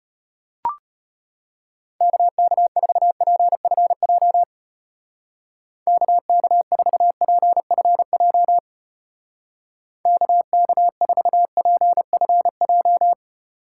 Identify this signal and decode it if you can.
Morse code, heard as a keyed tone, about 32 wpm, 700 Hz. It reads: KK4PFJ KK4PFJ KK4PFJ